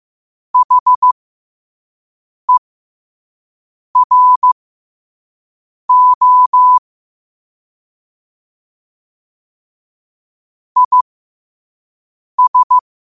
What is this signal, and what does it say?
Morse code, heard as a keyed tone, about 15 wpm, spaced as for 5 wpm, 1000 Hz. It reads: HERO IS
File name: qotdfiles / 5wpm